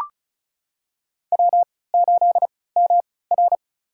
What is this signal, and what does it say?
Morse code, heard as a keyed tone, about 36 wpm, spaced as for 20 wpm, 690 Hz. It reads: W8MR
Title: USA Call Signs - 4 Characters - 3x Character Spacing - 35wpm